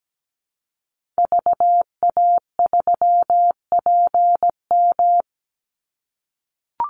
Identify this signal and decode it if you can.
Morse code, heard as a keyed tone, about 17 wpm, 700 Hz. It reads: VA3PM
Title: Call Signs - 17wpm